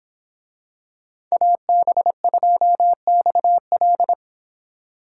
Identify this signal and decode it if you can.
Morse code, heard as a keyed tone, about 26 wpm, 700 Hz. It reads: AB2XL